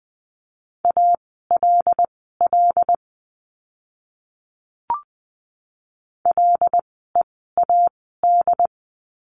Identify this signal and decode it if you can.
Morse code, heard as a keyed tone, about 20 wpm, spaced as for 15 wpm, 700 Hz. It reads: ALL LEAD
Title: 101 Lesson 17f - Letter u - Words 20wpm